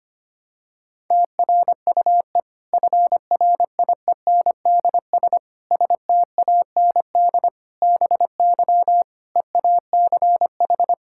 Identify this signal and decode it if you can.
Morse code, heard as a keyed tone, about 25 wpm, 700 Hz. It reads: TRUE FRIENDS STAND BY EACH
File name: practice / 25wpm / Difficult Sentences with Hints - 25wpm